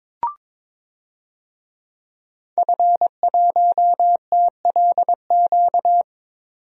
Morse code, heard as a keyed tone, about 22 wpm, 700 Hz. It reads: F1TLQ
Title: DXCC-100 Call Signs - 5 Characters - 22wpm